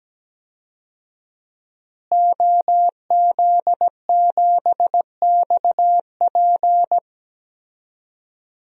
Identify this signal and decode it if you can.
Morse code, heard as a keyed tone, about 17 wpm, 700 Hz. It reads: OZ7XP
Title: DXCC-100 Call Signs - 5 Characters - Repeated 2x - 17wpm